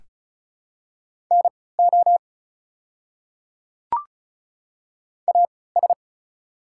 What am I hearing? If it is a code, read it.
Morse code, heard as a keyed tone, about 36 wpm, spaced as for 20 wpm, 700 Hz. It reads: NO AS
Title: 3x Character Spacing - Top 100 Words - 35wpm